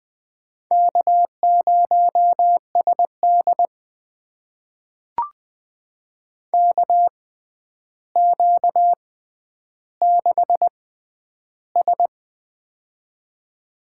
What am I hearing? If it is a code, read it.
Morse code, heard as a keyed tone, about 20 wpm, spaced as for 10 wpm, 700 Hz. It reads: K0SD KQ6S
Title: USA Call Signs - 4 Characters - 6x Character Spacing - 20wpm